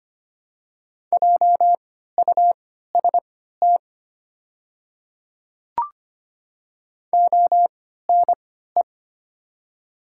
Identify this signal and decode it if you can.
Morse code, heard as a keyed tone, about 25 wpm, spaced as for 14 wpm, 700 Hz. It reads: JUST ONE
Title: 3x Character Spacing - Top 300 Words - 25wpm